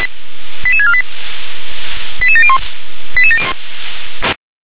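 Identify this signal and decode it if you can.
Five-tone selective call (ZVEI), heard as quick runs of five tones, tones of about 70 ms. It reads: 88648 88701
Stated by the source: Received on SDR.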